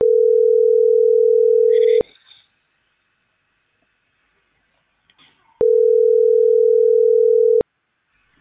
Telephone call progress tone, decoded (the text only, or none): ringback tone